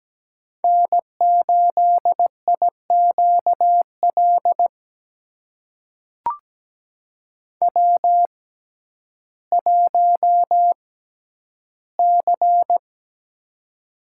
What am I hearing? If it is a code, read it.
Morse code, heard as a keyed tone, about 17 wpm, spaced as for 10 wpm, 700 Hz. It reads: N8IQL W1C
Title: USA Call Signs - 5 Characters - 6x Character Spacing - 17wpm